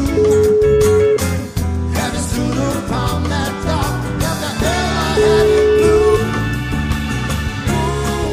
Ringback tone (Europe, 425 Hz), with noise.